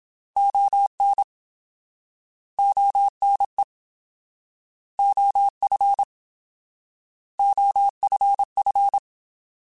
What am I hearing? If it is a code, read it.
Morse code, heard as a keyed tone, about 27 wpm, 800 Hz. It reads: ON ONE OF OFF